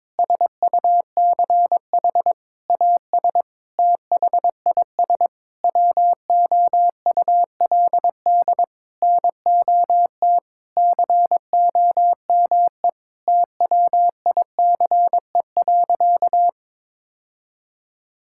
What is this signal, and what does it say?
Morse code, heard as a keyed tone, about 22 wpm, 700 Hz. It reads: SUCH AS THIS WOULD NOT COME TWICE.
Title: Sentences from Top 1600 Words 22wpm